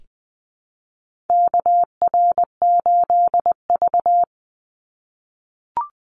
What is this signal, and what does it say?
Morse code, heard as a keyed tone, about 20 wpm, 700 Hz. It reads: KR8V
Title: Call Signs - 20wpm